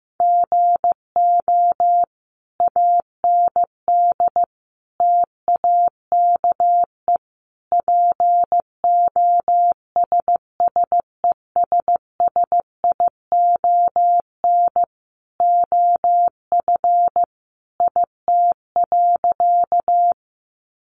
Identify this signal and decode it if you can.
Morse code, heard as a keyed tone, about 15 wpm, 700 Hz. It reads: GO AND TAKE POSSESSION OF IT.
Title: Sentences from Top 1400 Words 15wpm